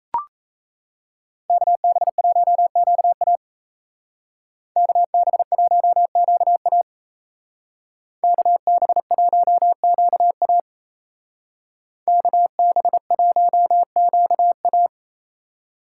Morse code, heard as a keyed tone, about 32 wpm, 700 Hz. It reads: KB1QA KB1QA KB1QA KB1QA
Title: Speed Racing - USA 2x2 Call Signs 28wpm